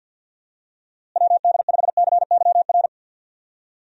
Morse code, heard as a keyed tone, about 52 wpm, 700 Hz. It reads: WD5CYR
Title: USA Call Signs - 6 Characters - Repeated 3x - 50wpm